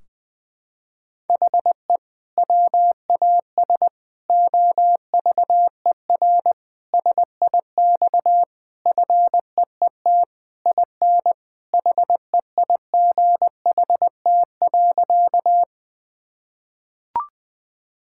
Morse code, heard as a keyed tone, about 20 wpm, 700 Hz. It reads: HE WAS OVER SIX FEET IN HEIGHT.
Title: Sentences from Top 1600 Words 20wpm